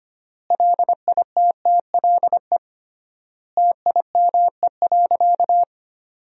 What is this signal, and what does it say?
Morse code, heard as a keyed tone, about 25 wpm, 700 Hz. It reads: LITTLE TIME.
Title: 3x Word-Spacing - Sentences from Top 200 Words - 25wpm